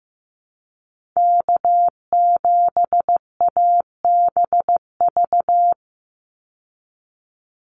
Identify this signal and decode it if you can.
Morse code, heard as a keyed tone, about 15 wpm, 700 Hz. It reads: K7ABV
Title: Call Signs Repeated 3x - 15wpm